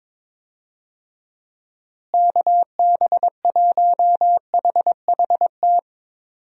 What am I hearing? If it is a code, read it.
Morse code, heard as a keyed tone, about 21 wpm, 700 Hz. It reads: KB1HHT
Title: USA Call Signs - 6 Characters - Repeated 3x - 22wpm